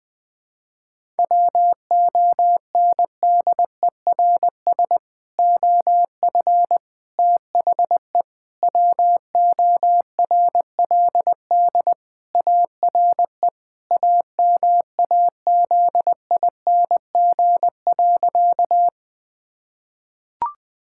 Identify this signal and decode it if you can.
Morse code, heard as a keyed tone, about 20 wpm, 700 Hz. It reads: WONDERS OF THE WORLD ARE AMAZING.
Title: Sentences with 3-Word Phrases - 20wpm